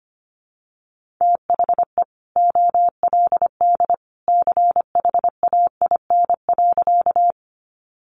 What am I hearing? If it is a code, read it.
Morse code, heard as a keyed tone, about 25 wpm, 700 Hz. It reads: THE OLD CHAIN.
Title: Sentences from Top 1500 Words 25wpm